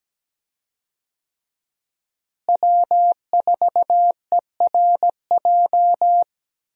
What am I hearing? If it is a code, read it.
Morse code, heard as a keyed tone, about 17 wpm, 700 Hz. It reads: W4ERJ
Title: USA Call Signs - 5 Characters - Repeated 3x - 17wpm